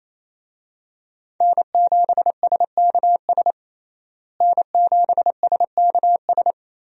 Morse code, heard as a keyed tone, about 28 wpm, 700 Hz. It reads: N7SKS N7SKS
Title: USA Call Signs - 5 Characters - Repeated 2x - 28wpm